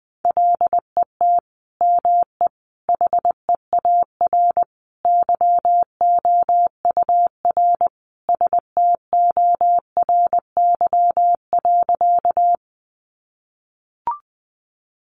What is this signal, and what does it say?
Morse code, heard as a keyed tone, about 19 wpm, 700 Hz. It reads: LET ME HEAR YOUR STORY.
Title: Sentences from Top 300 Words 20wpm